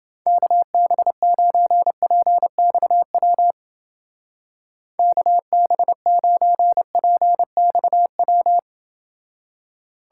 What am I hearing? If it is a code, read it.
Morse code, heard as a keyed tone, about 28 wpm, 700 Hz. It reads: KB9PXW KB9PXW